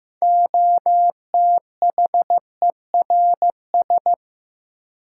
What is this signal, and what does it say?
Morse code, heard as a keyed tone, about 15 wpm, 700 Hz. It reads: OTHERS
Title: Sentences from Top 1000 Words - Review Part 2 - New Word Drill - 15wpm